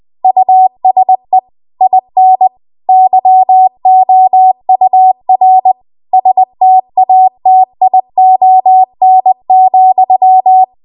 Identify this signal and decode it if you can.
Morse code, heard as a keyed tone, about 20 wpm, 750 Hz. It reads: USE IN YOUR STATION,
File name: Archive / 20 WPM / 20WPM